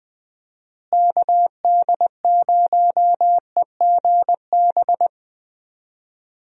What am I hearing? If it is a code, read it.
Morse code, heard as a keyed tone, about 20 wpm, 700 Hz. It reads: KD0EGB
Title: USA Call Signs - 6 Characters - Repeated 2x - 20wpm